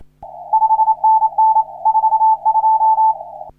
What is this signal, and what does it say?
Morse code, heard as a keyed tone, about 27 wpm, 880 Hz. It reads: 5NN 42
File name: CW / HA3OU RHOBCW REC